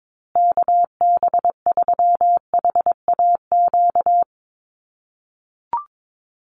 Morse code, heard as a keyed tone, about 22 wpm, 700 Hz. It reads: KB3HAQ
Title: USA Call Signs Repeated 2x 22wpm